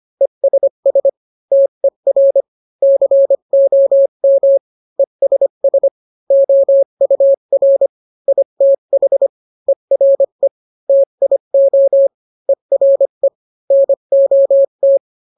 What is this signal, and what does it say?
Morse code, heard as a keyed tone, about 25 wpm, 550 Hz. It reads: ESS TER COM ESS OUR ITH ERE TIO ERE NOT